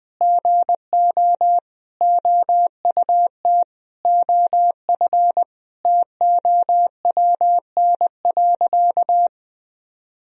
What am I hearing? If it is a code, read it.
Morse code, heard as a keyed tone, about 20 wpm, 700 Hz. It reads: GO OUT OF TOWN.
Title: Sentences from Top 1000 Words - Review Part 4 - 20wpm